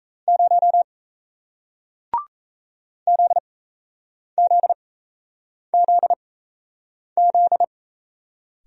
Morse code, heard as a keyed tone, about 38 wpm, spaced as for 8 wpm, 700 Hz. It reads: 0 ZZZZ